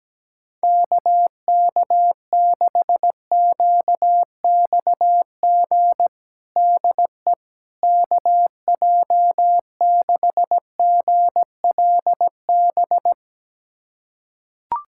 Morse code, heard as a keyed tone, about 17 wpm, 700 Hz. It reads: KK6QXG DE KJ6GLB
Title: USA Callsign DE Callsign - 6 Characters - 17wpm